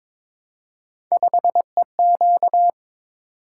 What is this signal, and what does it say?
Morse code, heard as a keyed tone, about 22 wpm, 700 Hz. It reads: 5EQ
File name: practice / 22wpm / Three Letter-Number Combinations - 22wpm